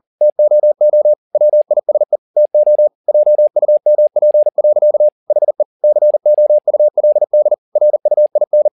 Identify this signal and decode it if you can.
Morse code, heard as a keyed tone, about 40 wpm, 600 Hz. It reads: TOO WISE TO JUMP. HE COULD RUIN